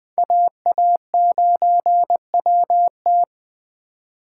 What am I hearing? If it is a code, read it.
Morse code, heard as a keyed tone, about 20 wpm, 700 Hz. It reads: AA9WT